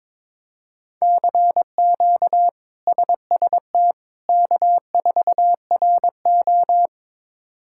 Morse code, heard as a keyed tone, about 22 wpm, 700 Hz. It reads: CQ SST K4RO
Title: K1USN SST QSO Practice - 22wpm